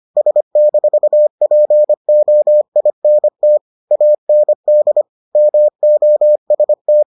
Morse code, heard as a keyed tone, about 25 wpm, 600 Hz. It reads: S-POINT AND MOST